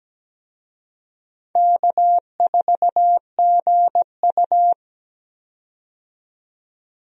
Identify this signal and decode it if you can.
Morse code, heard as a keyed tone, about 17 wpm, 700 Hz. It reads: K4GU
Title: USA Call Signs - 4 Characters - 17wpm